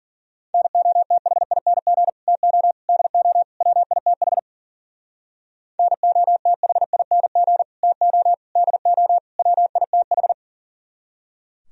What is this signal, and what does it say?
Morse code, heard as a keyed tone, about 54 wpm, 700 Hz. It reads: NOTHING TO DO WITH NOTHING TO DO WITH